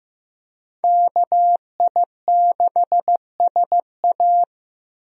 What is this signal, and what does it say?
Morse code, heard as a keyed tone, about 15 wpm, 700 Hz. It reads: KI6SA
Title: USA Call Signs - 5 Characters - Repeated 2x - 15wpm